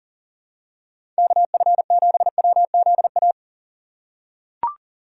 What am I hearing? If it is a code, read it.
Morse code, heard as a keyed tone, about 39 wpm, 700 Hz. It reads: KF7WZA
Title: USA Call Signs - 6 Characters - 2x Character Spacing - 40wpm